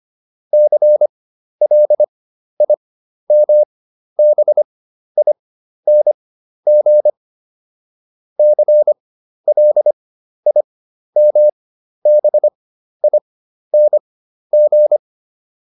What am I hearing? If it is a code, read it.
Morse code, heard as a keyed tone, about 25 wpm, spaced as for 12 wpm, 600 Hz. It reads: CLIMBING CLIMBING